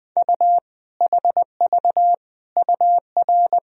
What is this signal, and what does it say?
Morse code, heard as a keyed tone, about 20 wpm, 700 Hz. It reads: U HV UR
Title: Sentences from Top 400 Words and Abbreviations 20wpm